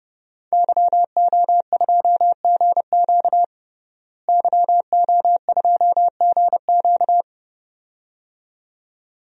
Morse code, heard as a keyed tone, about 30 wpm, 700 Hz. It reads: YO2GQ YO2GQ
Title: DXCC-100 Call Signs - 5 Characters - Repeated 2x - 30wpm